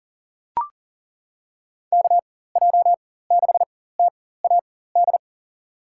Morse code, heard as a keyed tone, about 40 wpm, spaced as for 19 wpm, 690 Hz. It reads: KJ6TAD
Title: USA Call Signs - 4x Character Spacing - 40wpm